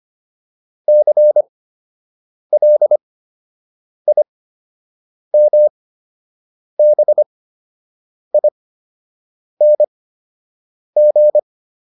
Morse code, heard as a keyed tone, about 25 wpm, spaced as for 7 wpm, 600 Hz. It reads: CLIMBING